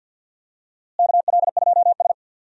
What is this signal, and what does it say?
Morse code, heard as a keyed tone, about 52 wpm, 700 Hz. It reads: KL2S
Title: USA Call Signs - 4 Characters - Repeated 2x - 50wpm